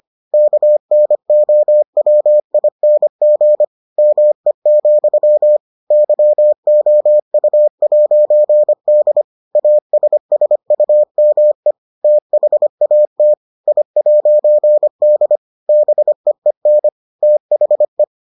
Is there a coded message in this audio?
Yes: Morse code from 0.3 to 18.0 s, heard as a keyed tone, about 25 wpm, 600 Hz. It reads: KNOWING ME, YOU'D ASSUME THAT I'D BEEN THE